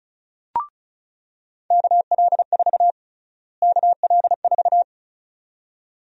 Morse code, heard as a keyed tone, about 34 wpm, 700 Hz. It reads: KL4 KL4
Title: USA Call Sign Prefixes - Repeated 2x - 35wpm